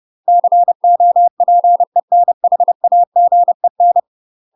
Morse code, heard as a keyed tone, about 30 wpm, 700 Hz. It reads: COPENHAGEN
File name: UK MASTER QSOs TRAINING / 30WPM / UK MASTER QSOs 0004